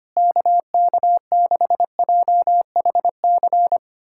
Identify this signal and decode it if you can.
Morse code, heard as a keyed tone, about 25 wpm, 700 Hz. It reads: KK6JHC
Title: USA Callsign DE Callsign 25wpm